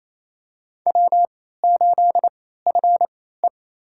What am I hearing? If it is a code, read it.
Morse code, heard as a keyed tone, about 28 wpm, spaced as for 16 wpm, 700 Hz. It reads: W8FE
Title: USA Call Signs - 4 Characters - 3x Character Spacing - 28wpm